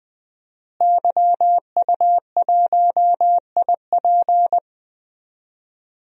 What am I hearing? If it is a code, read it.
Morse code, heard as a keyed tone, about 21 wpm, 700 Hz. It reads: YU1IP
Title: DXCC-100 Call Signs - 5 Characters - Repeated 3x - 20wpm